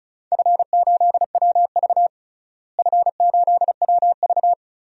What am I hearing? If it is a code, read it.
Morse code, heard as a keyed tone, about 34 wpm, 700 Hz. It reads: F8WV F8WV
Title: DXCC-100 Call Signs - 4 Characters - Repeated 3x - 35wpm